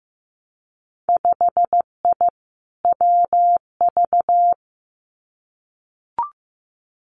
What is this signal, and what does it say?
Morse code, heard as a keyed tone, about 15 wpm, 700 Hz. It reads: 5I WV
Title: Winter Field Day Exchanges Encore 15wpm